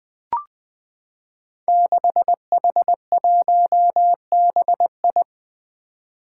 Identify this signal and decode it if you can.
Morse code, heard as a keyed tone, about 21 wpm, 700 Hz. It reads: 6H1BI
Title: Five Challenging Letter-Number Combinations 20wpm